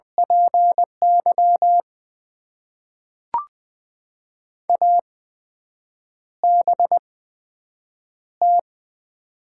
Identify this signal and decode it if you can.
Morse code, heard as a keyed tone, about 20 wpm, spaced as for 7 wpm, 700 Hz. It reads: PY ABT
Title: Top 90 QSO Elements - 8x Character Spacing - 20wpm